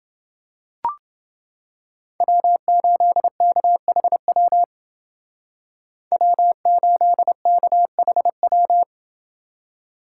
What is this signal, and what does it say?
Morse code, heard as a keyed tone, about 29 wpm, 700 Hz. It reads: W8KHW W8KHW